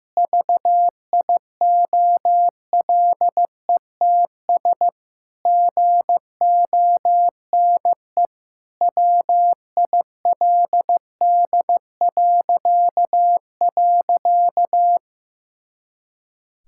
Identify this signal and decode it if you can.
Morse code, heard as a keyed tone, about 15 wpm, 700 Hz. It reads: VIOLETS GONE WILD..
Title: Pangram Challenge v2 - 15wpm